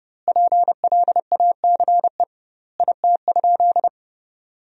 Morse code, heard as a keyed tone, about 30 wpm, 700 Hz. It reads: PLACE IT?
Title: 2x Word-Spacing - Sentences from Top 200 Words - 30wpm